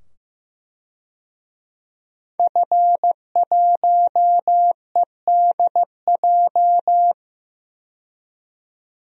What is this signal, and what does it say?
Morse code, heard as a keyed tone, about 15 wpm, 700 Hz. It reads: F1EDJ
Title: DXCC-100 Call Signs - 5 Characters - 15wpm